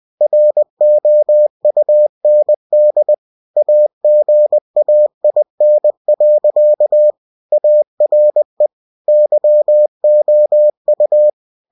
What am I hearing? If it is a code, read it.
Morse code, heard as a keyed tone, about 20 wpm, 600 Hz. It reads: ROUND AGAIN. ARE YOU
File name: cw-books / winnie-the-pooh / winnie-20wpm-page-0069